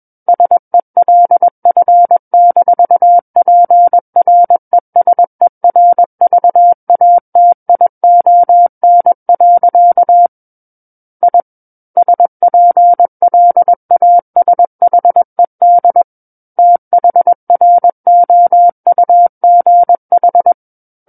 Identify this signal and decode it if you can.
Morse code, heard as a keyed tone, about 21 wpm, 700 Hz. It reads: SELF-PRESERVATION. I SPLASHED THROUGH
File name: Book One / 12-Chapter 12 - 21 WPM